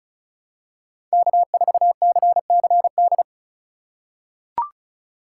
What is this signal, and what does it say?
Morse code, heard as a keyed tone, about 34 wpm, 700 Hz. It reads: K4CCD
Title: USA Call Signs - 5 Characters - 3x Character Spacing - 35wpm